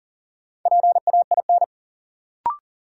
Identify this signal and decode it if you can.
Morse code, heard as a keyed tone, about 40 wpm, 700 Hz. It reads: PAIN
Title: Sentences from Top 1600 Words - New Word Drill - 40wpm